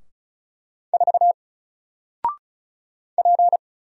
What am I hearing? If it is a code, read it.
Morse code, heard as a keyed tone, about 33 wpm, 700 Hz. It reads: 4 P
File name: Single-Letter-Number-35wpm